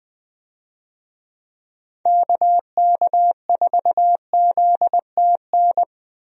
Morse code, heard as a keyed tone, about 21 wpm, 700 Hz. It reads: KK4ZTN
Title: USA Call Signs - 6 Characters - 20wpm